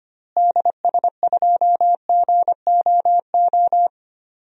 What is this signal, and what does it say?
Morse code, heard as a keyed tone, about 25 wpm, 700 Hz. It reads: DS2GOO
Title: DXCC-100 Call Signs - 6 Characters - Repeated 3x - 25wpm